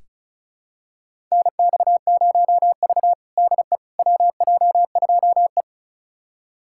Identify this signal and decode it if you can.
Morse code, heard as a keyed tone, about 36 wpm, 700 Hz. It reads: NX0V DE WJ2E
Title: USA Callsign DE Callsign - 4 Characters - 35wpm